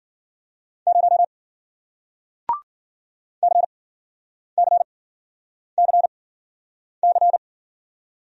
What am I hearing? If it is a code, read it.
Morse code, heard as a keyed tone, about 53 wpm, spaced as for 9 wpm, 700 Hz. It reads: 0 CCCC